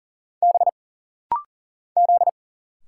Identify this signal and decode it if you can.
Morse code, heard as a keyed tone, about 41 wpm, 700 Hz. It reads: B Z
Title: Single Letters - Rapid-Fire 40wpm